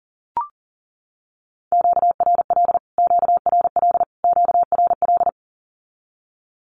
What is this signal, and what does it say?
Morse code, heard as a keyed tone, about 40 wpm, 700 Hz. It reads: QRL QRL QRL
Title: Q-Codes Repeated 3x 40wpm